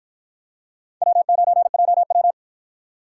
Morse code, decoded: W9JW

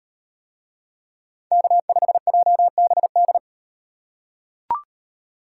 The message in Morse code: K5JBD

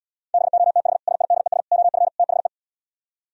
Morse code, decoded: BOTH SIDES OF THE

